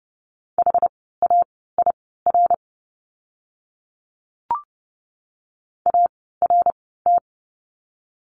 Morse code, heard as a keyed tone, about 30 wpm, spaced as for 17 wpm, 700 Hz. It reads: HAIR ART
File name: Lesson 17f - Letter u - Words 30wpm